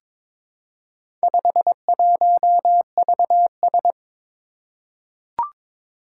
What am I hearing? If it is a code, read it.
Morse code, heard as a keyed tone, about 22 wpm, 690 Hz. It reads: 51VS
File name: practice / 22wpm / Four Letter-Number Combinations - 22wpm